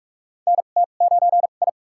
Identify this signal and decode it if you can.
Morse code, heard as a keyed tone, about 42 wpm, spaced as for 31 wpm, 700 Hz. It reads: NT9I
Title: USA Call Signs - 2x Character Spacing - 45wpm